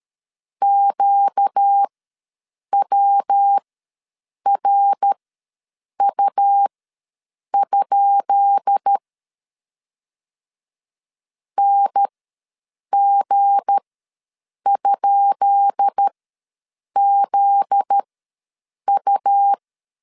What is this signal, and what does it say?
Morse code, heard as a keyed tone, about 13 wpm, spaced as for 7 wpm, 800 Hz. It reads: QWRU? NG?ZU
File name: LEZIONI / 10 SIGNS / 10B INT ws 12wpm 3527